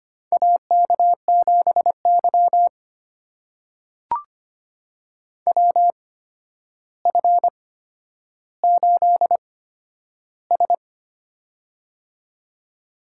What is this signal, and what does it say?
Morse code, heard as a keyed tone, about 25 wpm, spaced as for 11 wpm, 700 Hz. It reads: AK7Y WF8S
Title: USA Call Signs - 4 Characters - 8x Character Spacing - 25wpm